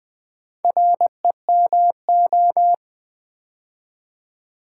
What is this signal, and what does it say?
Morse code, heard as a keyed tone, about 20 wpm, 690 Hz. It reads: REMO
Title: Build A Word - 2 Letters At A Time - 6-Letter Words - 20wpm